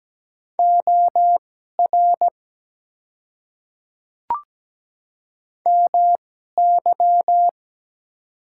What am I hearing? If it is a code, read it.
Morse code, heard as a keyed tone, about 17 wpm, spaced as for 12 wpm, 700 Hz. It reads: OR MY
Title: 2x Character Spacing - Top 100 Words - 17wpm